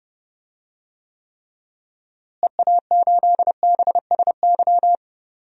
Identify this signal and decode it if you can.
Morse code, heard as a keyed tone, about 30 wpm, 700 Hz. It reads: EA8BSY
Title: DXCC-100 Call Signs - 6 Characters - Repeated 2x - 30wpm